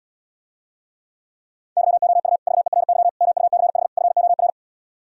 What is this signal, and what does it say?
Morse code, heard as a keyed tone, about 77 wpm, 700 Hz. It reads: YOU VERY MUCH FOR